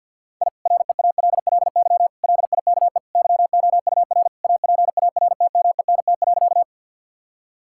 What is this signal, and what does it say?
Morse code, heard as a keyed tone, about 50 wpm, 700 Hz. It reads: I REALLY LIKE YOUR APARTMENT.
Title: Sentences from Top 1000 Words - Review Part 6 - 50wpm